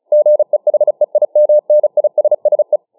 Morse code, heard as a keyed tone, about 35 wpm, 600 Hz. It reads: GEHEIMNISSE
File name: dk5ke / Morsemogeln 35WpM